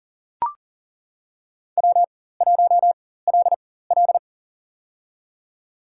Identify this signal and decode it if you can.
Morse code, heard as a keyed tone, about 39 wpm, spaced as for 19 wpm, 700 Hz. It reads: W1LL
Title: USA Call Signs - 4 Characters - 4x Character Spacing - 40wpm